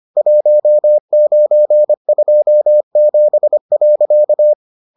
Morse code, heard as a keyed tone, about 25 wpm, 600 Hz. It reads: 1927.